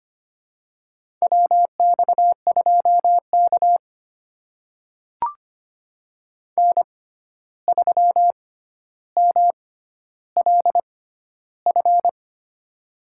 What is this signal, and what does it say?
Morse code, heard as a keyed tone, about 25 wpm, spaced as for 12 wpm, 700 Hz. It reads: WX2K N3MLF
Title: USA Call Signs - 6x Character Spacing - 25wpm